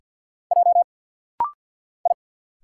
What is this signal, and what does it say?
Morse code, heard as a keyed tone, about 50 wpm, 690 Hz. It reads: J I
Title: Single Letters - Rapid-Fire 50wpm